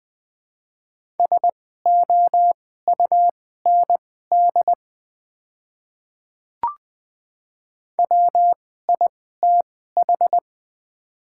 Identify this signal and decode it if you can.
Morse code, heard as a keyed tone, about 20 wpm, spaced as for 14 wpm, 700 Hz. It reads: SOUND WITH
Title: 2x Character Spacing - Top 100 Words - 20wpm